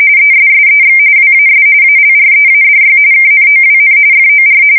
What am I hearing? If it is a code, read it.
RTTY, about 45.45 baud. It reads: ESSEX WAS ALREADY KNOWN FOR
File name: fsk170 / usnavy2